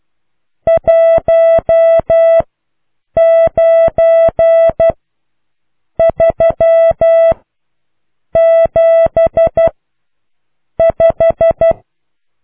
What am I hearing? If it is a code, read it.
Morse code, heard as a keyed tone, about 12 wpm, spaced as for 6 wpm, 660 Hz. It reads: 19375